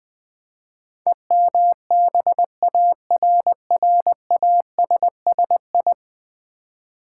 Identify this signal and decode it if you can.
Morse code, heard as a keyed tone, about 20 wpm, 700 Hz. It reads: EMBARRASSI
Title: Build A Word - 5 Letters At A Time - 12-Letter Words - 20wpm